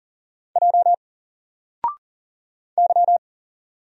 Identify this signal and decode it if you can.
Morse code, heard as a keyed tone, about 40 wpm, 700 Hz. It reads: J Y
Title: Single Letter-Number 40wpm